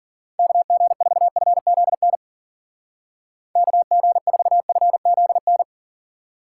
Morse code, heard as a keyed tone, about 49 wpm, 700 Hz. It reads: KG4FZN KG4FZN
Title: Speed Racing - USA 2x3 Call Signs 40wpm